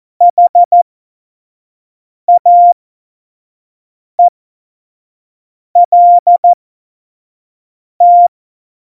Morse code, heard as a keyed tone, about 14 wpm, spaced as for 5 wpm, 700 Hz. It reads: HAELT
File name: Morsekurs / Witze / Witz 09. Januar 2018, 25 F70